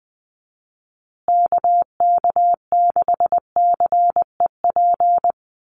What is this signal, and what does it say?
Morse code, heard as a keyed tone, about 20 wpm, 700 Hz. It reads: KK6CEP
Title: USA Call Signs Repeated 3x 20wpm